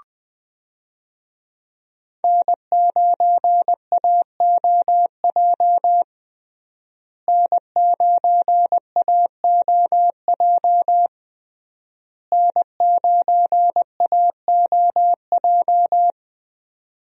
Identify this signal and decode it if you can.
Morse code, heard as a keyed tone, about 20 wpm, 700 Hz. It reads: N9AOJ N9AOJ N9AOJ
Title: USA Call Signs - 5 Characters - Repeated 3x - 20wpm